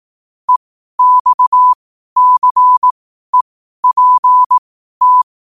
Morse code, heard as a keyed tone, about 18 wpm, spaced as for 12 wpm, 1000 Hz. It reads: EXCEPT
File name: qotdfiles / 12wpm